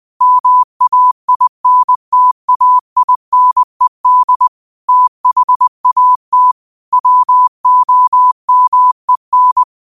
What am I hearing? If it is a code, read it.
Morse code, heard as a keyed tone, about 20 wpm, 1000 Hz. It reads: MAINTAINED THAT WOMEN
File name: qotdfiles / 20wpm